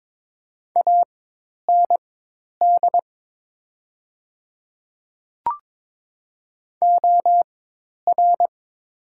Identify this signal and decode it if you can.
Morse code, heard as a keyed tone, about 22 wpm, spaced as for 10 wpm, 700 Hz. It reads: AND OR
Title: 4x Character Spacing - Top 100 Words - 22wpm